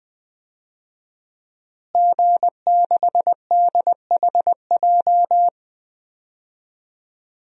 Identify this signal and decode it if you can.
Morse code, heard as a keyed tone, about 20 wpm, 700 Hz. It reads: G6DHJ
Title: DXCC-100 Call Signs - 5 Characters - Repeated 3x - 20wpm